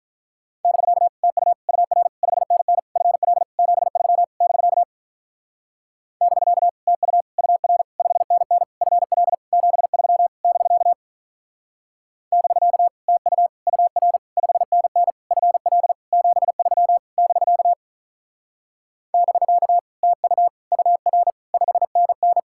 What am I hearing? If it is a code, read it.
Morse code, heard as a keyed tone, about 43 wpm, 700 Hz. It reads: <BK> TU UR 5NN FL 73 <BK> <BK> TU UR 5NN FL 73 <BK> <BK> TU UR 5NN FL 73 <BK> <BK> TU UR 5NN
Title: Speed-Racing Common QSO Phrases - 35wpm